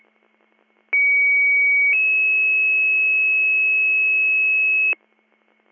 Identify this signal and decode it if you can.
Two-tone page, heard as a one-second tone, then a three-second tone, about 2250 then 2600 Hz, with noise. It is heard from 0.9 to 4.9 s.